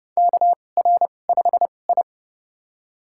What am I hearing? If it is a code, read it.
Morse code, heard as a keyed tone, about 30 wpm, spaced as for 22 wpm, 700 Hz. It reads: KR5I
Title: USA Call Signs - 4 Characters - 2x Character Spacing - 30wpm